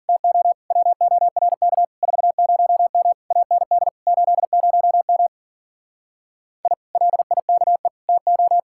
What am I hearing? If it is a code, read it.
Morse code, heard as a keyed tone, about 49 wpm, 700 Hz. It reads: TO WORK 40M AND 80M I LIKE TO
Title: Speed-Racing Common QSO Phrases - 40wpm